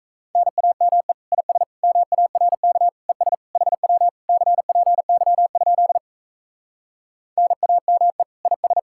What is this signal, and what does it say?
Morse code, heard as a keyed tone, about 37 wpm, 700 Hz. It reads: NAME IS MARK ES HW CPY? NAME IS